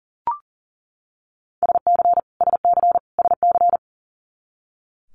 Morse code, heard as a keyed tone, about 40 wpm, 690 Hz. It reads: SC SC SC
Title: Multi-letter Phonemes Repeated 3x 40wpm